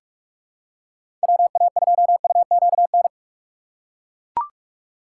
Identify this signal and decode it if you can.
Morse code, heard as a keyed tone, about 42 wpm, 700 Hz. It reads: WA2UQN
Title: USA Call Signs - 6 Characters - 6x Character Spacing - 45wpm